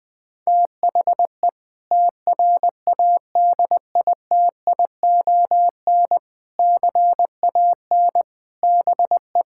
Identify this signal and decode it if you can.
Morse code, heard as a keyed tone, about 20 wpm, 700 Hz. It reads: THE TRADITION CAN BE
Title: Sentences with 6-Word Phrases - 20wpm